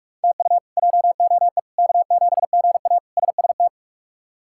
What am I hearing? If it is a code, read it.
Morse code, heard as a keyed tone, about 46 wpm, 700 Hz. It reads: TU JOE K7GA SST
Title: K1USN SST QSO Practice - 45wpm